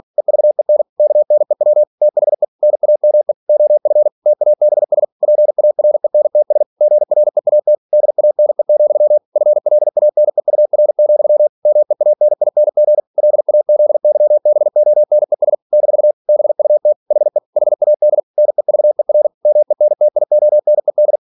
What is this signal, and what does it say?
Morse code, heard as a keyed tone, about 46 wpm, 600 Hz. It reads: EVEN KNEW THE NAME OF TABS PARENTS GREAT DANE, FLANEUR, MEANING LAZYBONES - BUT HE HAD NEVER MENTIONED
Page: Never 47wpm